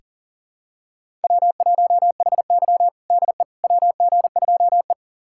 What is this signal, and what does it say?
Morse code, heard as a keyed tone, about 37 wpm, 700 Hz. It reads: W1HY DE WG2E
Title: USA Callsign DE Callsign - 4 Characters - 40wpm